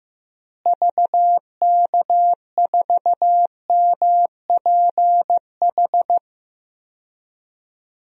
Morse code, heard as a keyed tone, about 16 wpm, 700 Hz. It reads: VK4MPH
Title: DXCC-100 Call Signs - 6 Characters - Repeated 3x - 15wpm